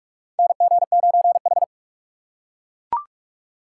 Morse code, heard as a keyed tone, about 41 wpm, 700 Hz. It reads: NG9H
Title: USA Call Signs - 4 Characters - 3x Character Spacing - 45wpm